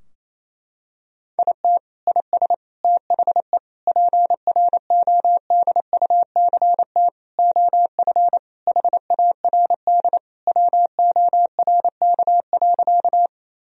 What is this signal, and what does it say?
Morse code, heard as a keyed tone, about 28 wpm, 700 Hz. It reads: IT IS THE PRODUCT OF HARD WORK.
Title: Sentences from Top 1000 Words - Review Part 2 - 28wpm